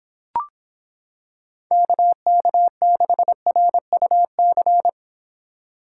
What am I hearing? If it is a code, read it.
Morse code, heard as a keyed tone, about 26 wpm, 700 Hz. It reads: KK6RUC